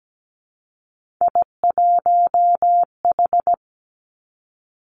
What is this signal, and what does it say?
Morse code, heard as a keyed tone, about 17 wpm, 700 Hz. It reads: I1H
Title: Three Challenging Letter-Number Combinations 17wpm